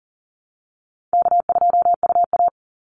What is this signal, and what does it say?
Morse code, heard as a keyed tone, about 37 wpm, 700 Hz. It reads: K2UA
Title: Call Signs Repeated 2x - 40wpm